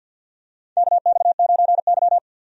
Morse code, heard as a keyed tone, about 52 wpm, 700 Hz. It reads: KX9Y
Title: USA Call Signs - 4 Characters - 50wpm